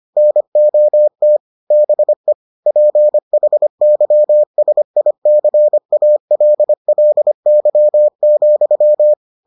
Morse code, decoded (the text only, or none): NOT BE PHYSICALLY,